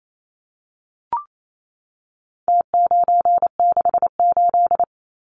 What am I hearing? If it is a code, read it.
Morse code, heard as a keyed tone, about 28 wpm, 700 Hz. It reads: T968
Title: Numbers - Contest Style - 4-Digit 28wpm